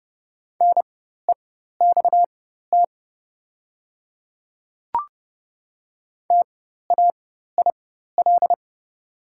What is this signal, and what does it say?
Morse code, heard as a keyed tone, about 30 wpm, spaced as for 14 wpm, 690 Hz. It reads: NEXT TAIL
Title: 4x Character Spacing - Top 500 Words - 30wpm